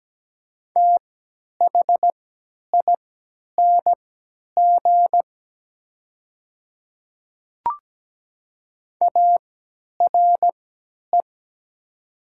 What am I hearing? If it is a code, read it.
Morse code, heard as a keyed tone, about 17 wpm, spaced as for 10 wpm, 700 Hz. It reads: THING ARE
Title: 3x Character Spacing - Top 200 Words - 17wpm